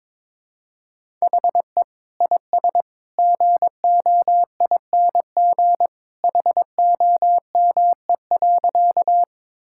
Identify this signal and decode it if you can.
Morse code, heard as a keyed tone, about 22 wpm, 700 Hz. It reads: HE IS GOING HOME.
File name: practice / 22wpm / Sentences with 2-Word Phrases - 22wpm